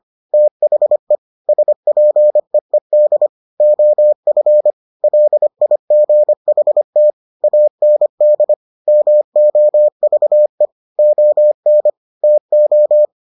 Morse code, heard as a keyed tone, about 25 wpm, 600 Hz. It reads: THE SPEED OF LIGHT AND MOVE ON TO